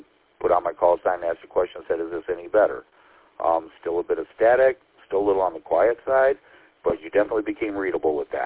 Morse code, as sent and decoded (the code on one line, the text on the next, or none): none